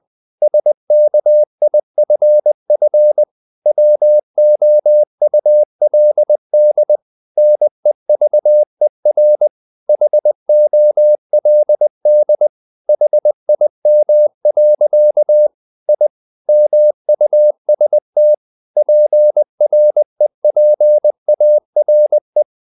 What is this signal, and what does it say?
Morse code, heard as a keyed tone, about 20 wpm, 600 Hz. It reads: SKIFF WOULD NEVER HOLD HIM. I MUST PREPARE
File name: the old man and the sea part 2 20wpm